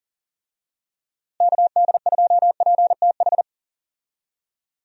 Morse code, heard as a keyed tone, about 37 wpm, 700 Hz. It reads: KD2PTH
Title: USA Call Signs - 6 Characters - Repeated 3x - 40wpm